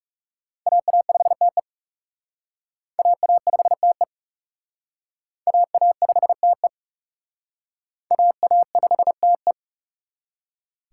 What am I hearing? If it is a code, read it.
Morse code, heard as a keyed tone, about 36 wpm, 700 Hz. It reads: AA5TE AA5TE AA5TE AA5TE
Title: Speed Racing - USA 2x2 Call Signs 30wpm